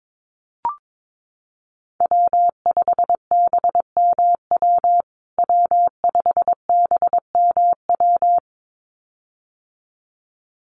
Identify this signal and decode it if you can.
Morse code, heard as a keyed tone, about 22 wpm, 700 Hz. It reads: W5BMW W5BMW
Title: USA Call Signs Repeated 2x 22wpm